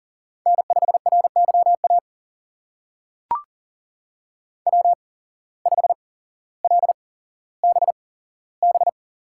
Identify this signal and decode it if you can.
Morse code, heard as a keyed tone, about 40 wpm, spaced as for 17 wpm, 700 Hz. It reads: N5RYA W5LBB
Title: USA Call Signs - 5 Characters - 8x Character Spacing - 40wpm